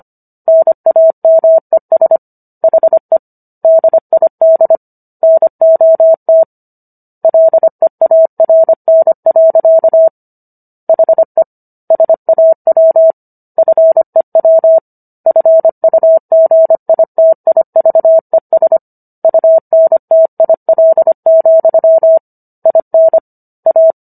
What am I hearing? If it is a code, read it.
Morse code, heard as a keyed tone, about 25 wpm, 650 Hz. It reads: NAMES HE DID NOT LEARN. HE SAW FEW FUGITIVES UNTIL, IN A